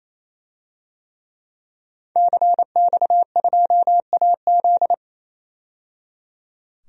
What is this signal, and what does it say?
Morse code, heard as a keyed tone, about 28 wpm, 700 Hz. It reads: CX2AZ
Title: DXCC-100 Call Signs - 5 Characters - 28wpm